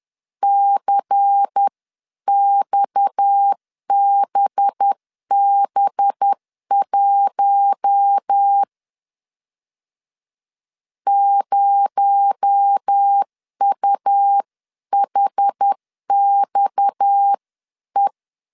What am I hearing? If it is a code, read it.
Morse code, heard as a keyed tone, about 10 wpm, 800 Hz. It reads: CXBB1 0UHXE
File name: CW / DET / 1 6 1500C S C 10wpm 4252